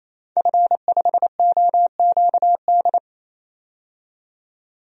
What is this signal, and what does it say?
Morse code, heard as a keyed tone, about 28 wpm, 700 Hz. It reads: F5OQD
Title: DXCC-100 Call Signs - 5 Characters - 28wpm